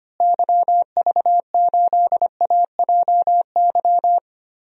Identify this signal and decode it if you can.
Morse code, heard as a keyed tone, about 25 wpm, 700 Hz. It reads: YV8AJY
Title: DXCC-100 Call Signs - 6 Characters - Repeated 3x - 25wpm